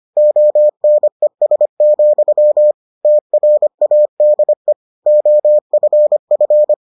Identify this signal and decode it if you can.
Morse code, heard as a keyed tone, about 25 wpm, 600 Hz. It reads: ONES, TRADE OFF